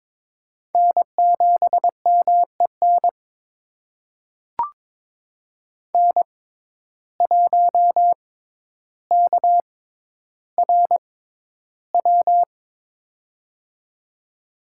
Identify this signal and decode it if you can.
Morse code, heard as a keyed tone, about 22 wpm, spaced as for 11 wpm, 700 Hz. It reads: N7MEN N1KRW
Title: USA Call Signs - 5 Characters - 6x Character Spacing - 22wpm